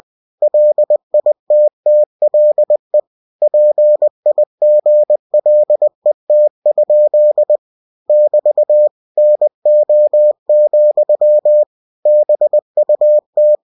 Morse code, heard as a keyed tone, about 20 wpm, 600 Hz. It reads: LITTLE PIGLET? <BT> NO, BUT
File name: cw-books / winnie-the-pooh / winnie-20wpm-page-0055